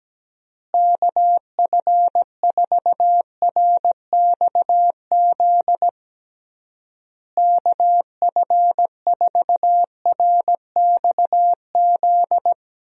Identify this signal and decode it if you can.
Morse code, heard as a keyed tone, about 17 wpm, 700 Hz. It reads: KF4RXZ KF4RXZ
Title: USA Call Signs - 6 Characters - Repeated 3x - 17wpm